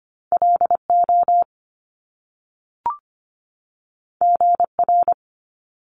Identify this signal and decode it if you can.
Morse code, heard as a keyed tone, about 25 wpm, 700 Hz. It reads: L O G R
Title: Top Two Letter Combinations in Words 25wpm